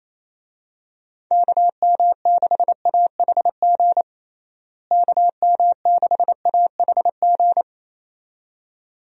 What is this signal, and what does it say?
Morse code, heard as a keyed tone, about 28 wpm, 700 Hz. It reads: KM6AHG KM6AHG
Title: USA Call Signs - 6 Characters - Repeated 2x - 28wpm